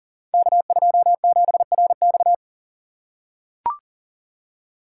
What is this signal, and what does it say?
Morse code, heard as a keyed tone, about 40 wpm, 700 Hz. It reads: K27RX
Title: Code Groups - 5 Letter:Numbers - 40wpm